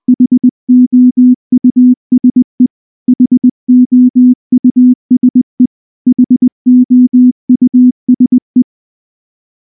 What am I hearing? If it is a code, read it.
Morse code, heard as a keyed tone, about 19 wpm, 250 Hz. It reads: HOUSE HOUSE HOUSE
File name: morse / morse-training-20WPM-10